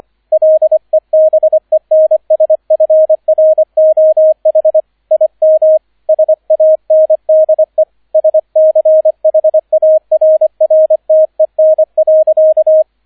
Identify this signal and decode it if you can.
Morse code, heard as a keyed tone, about 25 wpm, 630 Hz, with radio noise. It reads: LEBENSFROH IM SANDE SCHARRTEN.